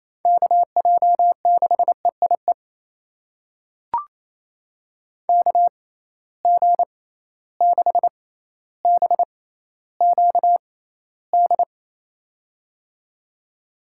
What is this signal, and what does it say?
Morse code, heard as a keyed tone, about 28 wpm, spaced as for 14 wpm, 700 Hz. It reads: KJ6EIE KG6BQD
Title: USA Call Signs - 6x Character Spacing - 28wpm